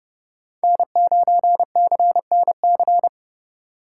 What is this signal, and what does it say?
Morse code, heard as a keyed tone, about 30 wpm, 700 Hz. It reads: N9CNC
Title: USA Call Signs - 2x Character Spacing - 30wpm